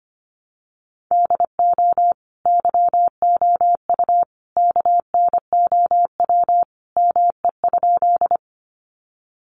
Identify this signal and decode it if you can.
Morse code, heard as a keyed tone, about 25 wpm, 700 Hz. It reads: DO YOU KNOW ME?
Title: Sentences from Top 200 Words 25wpm